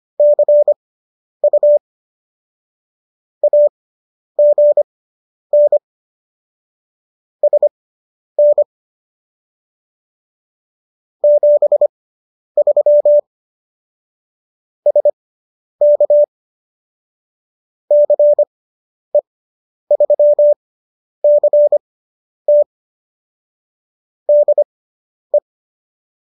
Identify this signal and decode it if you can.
Morse code, heard as a keyed tone, about 25 wpm, spaced as for 10 wpm, 590 Hz. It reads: CU AGN SN 73 SK CE3CT DE